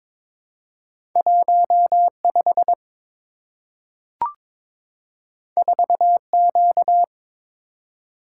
Morse code, heard as a keyed tone, about 22 wpm, 700 Hz. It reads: 15 4Q
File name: practice / 22wpm / Two Letter-Number Combinations - 22wpm